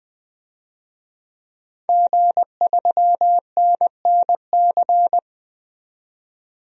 Morse code, heard as a keyed tone, about 20 wpm, 700 Hz. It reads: G3NNC
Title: DXCC-100 Call Signs - 5 Characters - 20wpm